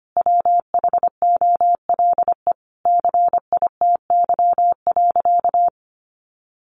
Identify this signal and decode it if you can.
Morse code, heard as a keyed tone, about 25 wpm, 700 Hz. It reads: WHOLE CITY.
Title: Sentences from Top 1400 Words 25wpm